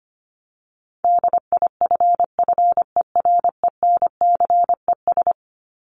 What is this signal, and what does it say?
Morse code, heard as a keyed tone, about 25 wpm, 700 Hz. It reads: DIFFERENCES
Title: ES:S Suffix - Top 100 Words with Base Words 25wpm